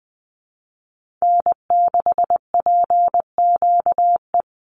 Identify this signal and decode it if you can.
Morse code, heard as a keyed tone, about 20 wpm, 700 Hz. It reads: N6PQE
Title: USA Call Signs 20wpm